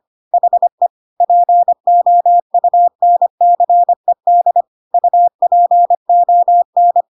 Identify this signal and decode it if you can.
Morse code, heard as a keyed tone, about 25 wpm, 700 Hz. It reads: HE POUNCED UPON
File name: Pinocchio / Pinocchio ch5 25 25 ik1hge